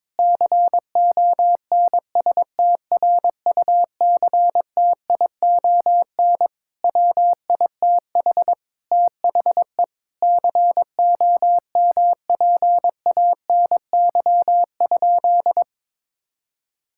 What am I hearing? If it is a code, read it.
Morse code, heard as a keyed tone, about 22 wpm, 700 Hz. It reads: CONSTRUCTION WITH THE COMPANY?
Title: Sentences from Top 1600 Words 22wpm